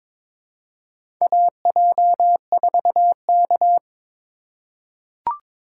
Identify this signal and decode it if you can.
Morse code, heard as a keyed tone, about 22 wpm, 700 Hz. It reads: AJ4K
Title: USA Call Signs - 4 Characters - 2x Character Spacing - 22wpm